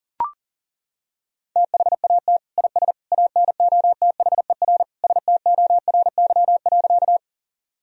Morse code, heard as a keyed tone, about 40 wpm, 700 Hz. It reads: THAT IS ANOTHER STORY.
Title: Sentences from Top 1000 Words - Review Part 1 - 40wpm